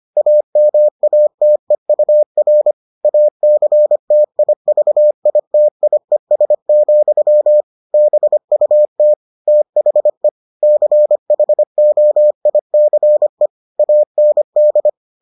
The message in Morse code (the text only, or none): AMATEUR ACTIVITIES, BUT THE CHOICE AND